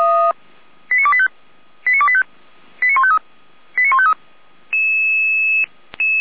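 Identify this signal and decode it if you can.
Five-tone selective call (ZVEI), heard as quick runs of five tones, tones of about 70 ms. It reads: 89285 89285 89153 89153
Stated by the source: ZVEI five-tone call 89025 received by SDR.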